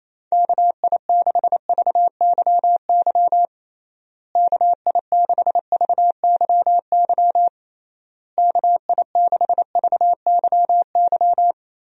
Morse code, heard as a keyed tone, about 27 wpm, 700 Hz. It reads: KI6VYY KI6VYY KI6VYY
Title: USA Call Signs - 6 Characters - Repeated 3x - 28wpm